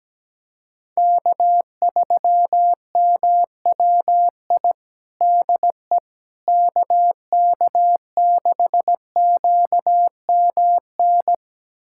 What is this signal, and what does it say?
Morse code, heard as a keyed tone, about 17 wpm, 700 Hz. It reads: K3MWI DE KK6QMN
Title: USA Callsign DE Callsign 17wpm